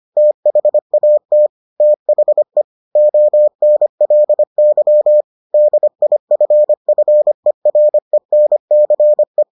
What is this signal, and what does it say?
Morse code, heard as a keyed tone, about 25 wpm, 600 Hz. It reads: THAT THE ONLY DIFFERENCE